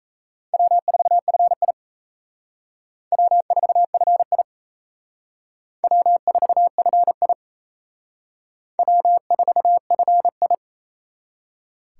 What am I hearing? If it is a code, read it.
Morse code, heard as a keyed tone, about 34 wpm, 700 Hz. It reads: W4FI W4FI W4FI W4FI
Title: Speed Racing - USA 1x2 Call Signs 28wpm